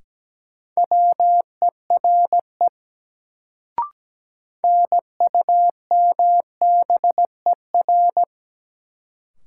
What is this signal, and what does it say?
Morse code, heard as a keyed tone, about 17 wpm, 700 Hz. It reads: WERE NUMBER
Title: Sentences from Top 100 Words - New Word Drill - 17wpm